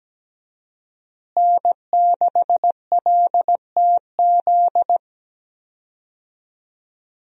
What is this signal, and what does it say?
Morse code, heard as a keyed tone, about 17 wpm, 700 Hz. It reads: N6LTZ
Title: USA Call Signs - 5 Characters - Repeated 3x - 17wpm